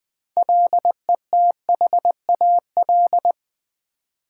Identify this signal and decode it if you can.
Morse code, heard as a keyed tone, about 20 wpm, 700 Hz. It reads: LETHAL
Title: Build A Word - 3 Letters At A Time - 6-Letter Words - 20wpm